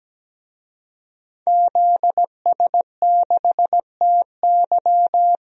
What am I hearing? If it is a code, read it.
Morse code, heard as a keyed tone, about 17 wpm, 700 Hz. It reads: ZS6TY
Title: DXCC-100 Call Signs - 5 Characters - 17wpm